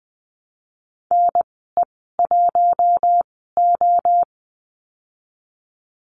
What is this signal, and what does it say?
Morse code, heard as a keyed tone, about 20 wpm, spaced as for 15 wpm, 700 Hz. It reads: NE1O
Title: 042 Lesson 8g - Number 1 - Call Signs 20wpm